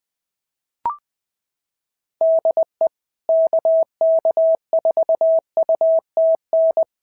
Morse code, heard as a keyed tone, about 20 wpm, 650 Hz. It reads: DE KK4UTN
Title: SOTA QSO Practice 20wpm